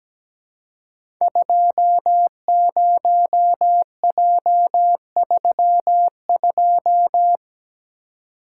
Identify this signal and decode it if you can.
Morse code, heard as a keyed tone, about 17 wpm, 700 Hz. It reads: 20J32
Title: Code Groups - 5 Letter:Numbers - 17wpm